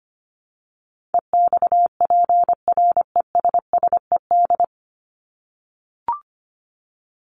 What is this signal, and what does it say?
Morse code, heard as a keyed tone, about 25 wpm, 700 Hz. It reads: EXPRESSED
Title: ED Suffix - Top 100 Words with Base Words 25wpm